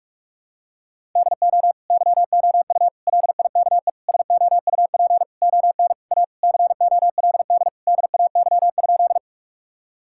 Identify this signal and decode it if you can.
Morse code, heard as a keyed tone, about 46 wpm, 700 Hz. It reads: DO YOU LIKE SOUP ON A COLD DAY?
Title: Sentences from Top 1000 Words - Review Part 6 - 45wpm